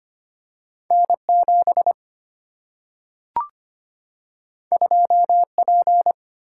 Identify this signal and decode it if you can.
Morse code, heard as a keyed tone, about 25 wpm, 700 Hz. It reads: N7 2P
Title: Single-Letter Single-Number 25wpm